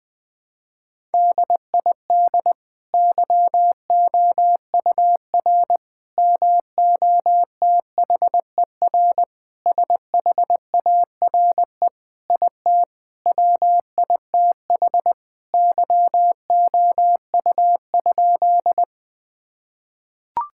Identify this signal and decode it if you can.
Morse code, heard as a keyed tone, about 19 wpm, 700 Hz. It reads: DID YOUR MOTHER SHARE IT WITH YOU?
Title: Sentences from Top 1000 Words - Review Part 6 - 20wpm